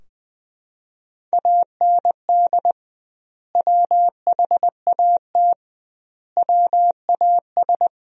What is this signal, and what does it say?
Morse code, heard as a keyed tone, about 20 wpm, 700 Hz. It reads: AND WHAT WAS
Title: 2x Word-Spacing - Sentences from Top 200 Words - 20wpm